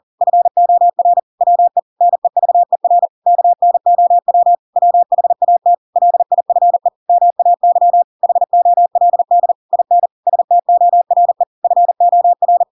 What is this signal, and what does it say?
Morse code, heard as a keyed tone, about 40 wpm, 700 Hz. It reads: FOR WE NEVER KNOW WHAT LIFE MAY HOLD IN STORE FOR
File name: Pinocchio / Pinocchio ch7 40 40 ik1hge